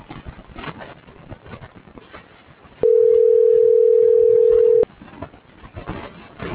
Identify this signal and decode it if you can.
Ringback tone (North America), 440 + 480 Hz.
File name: CENTRAL GENERAL / out tania.lemus